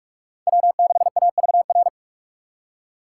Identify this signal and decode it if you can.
Morse code, heard as a keyed tone, about 43 wpm, 700 Hz. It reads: W6AVR